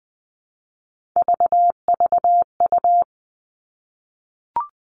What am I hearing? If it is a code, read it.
Morse code, heard as a keyed tone, about 20 wpm, 700 Hz. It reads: VVU
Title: Three Challenging Letter-Number Combinations 20wpm